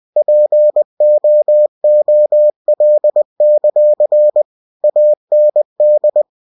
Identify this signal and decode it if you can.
Morse code, heard as a keyed tone, about 20 wpm, 600 Hz. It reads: POOL; AND
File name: cw-books / winnie-the-pooh / winnie-20wpm-page-0087